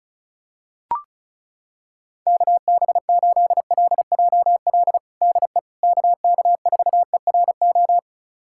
Morse code, heard as a keyed tone, about 34 wpm, 700 Hz. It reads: KB8LJL DE KK4ERO
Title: USA Callsign DE Callsign 35wpm